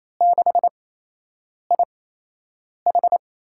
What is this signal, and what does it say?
Morse code, heard as a keyed tone, about 28 wpm, spaced as for 8 wpm, 700 Hz. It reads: 6IH
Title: USA Call Signs - 4 Characters - 8x Character Spacing - 28wpm